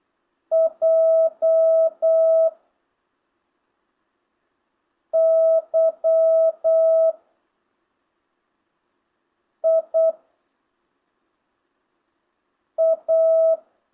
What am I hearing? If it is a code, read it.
Morse code, heard as a keyed tone, about 8 wpm, 660 Hz. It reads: J Y I A
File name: jh8cbh / 2025cw